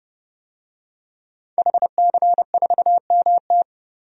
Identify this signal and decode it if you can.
Morse code, heard as a keyed tone, about 31 wpm, 700 Hz. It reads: HC4MT
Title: DXCC-100 Call Signs - 5 Characters - 30wpm